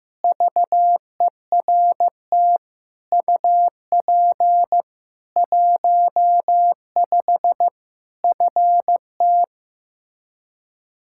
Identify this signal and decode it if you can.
Morse code, heard as a keyed tone, about 15 wpm, 700 Hz. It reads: VERT UP 15 FT
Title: K1USN SST QSO Practice 15wpm